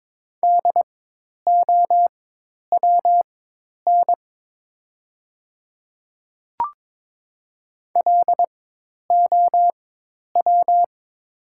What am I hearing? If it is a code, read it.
Morse code, heard as a keyed tone, about 22 wpm, spaced as for 10 wpm, 700 Hz. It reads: DOWN LOW
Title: 4x Character Spacing - Top 200 Words - 22wpm